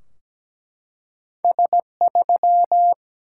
Morse code, heard as a keyed tone, about 17 wpm, 700 Hz. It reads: S3
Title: Single-Letter Single-Number 17wpm